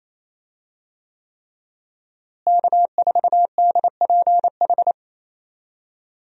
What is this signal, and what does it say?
Morse code, heard as a keyed tone, about 28 wpm, 700 Hz. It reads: K4DPH
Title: USA Call Signs - 5 Characters - Repeated 3x - 28wpm